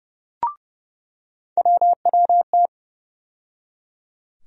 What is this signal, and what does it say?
Morse code, heard as a keyed tone, about 30 wpm, 700 Hz. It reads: WWT
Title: USA Call Sign Postfixes - 3 Letters 30wpm